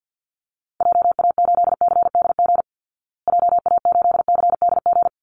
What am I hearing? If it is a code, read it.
Morse code, heard as a keyed tone, about 50 wpm, 700 Hz. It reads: JA8CDG JA8CDG
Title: Call Signs Repeated 3x - 50wpm